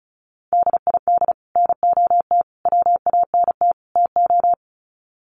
Morse code, heard as a keyed tone, about 34 wpm, 700 Hz. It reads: DID NOT WANT TO
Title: Sets of 4 Words 35wpm